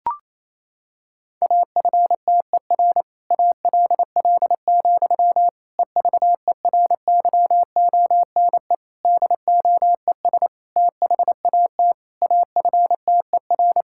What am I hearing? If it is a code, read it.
Morse code, heard as a keyed tone, about 28 wpm, 700 Hz. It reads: AFTER ALL, EVERYONE DOES THAT AFTER
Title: Sentences from Top 1000 Words - Review Part 5 - 28wpm